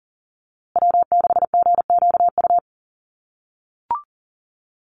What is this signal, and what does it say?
Morse code, heard as a keyed tone, about 40 wpm, 700 Hz. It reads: W6GQU
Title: USA Call Signs Repeated 2x 40wpm